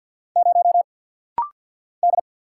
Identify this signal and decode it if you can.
Morse code, heard as a keyed tone, about 50 wpm, 700 Hz. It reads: 0 D
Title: Single Letter-Number Alt - Rapid-Fire 50wpm